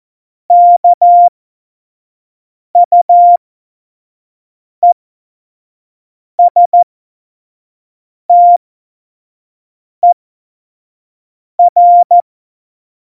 Morse code, heard as a keyed tone, about 14 wpm, spaced as for 5 wpm, 700 Hz. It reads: KUESTER
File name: Morsekurs / Witze / Witz 01. Dezember, 2020 25 F70